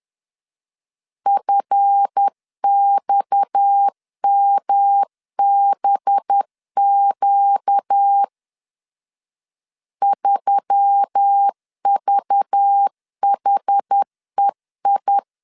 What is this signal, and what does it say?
Morse code, heard as a keyed tone, about 10 wpm, 800 Hz. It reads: FXMBQ 3VHEI
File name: LEZIONI / 7 GF9Z / 7 GF 10wpm 2514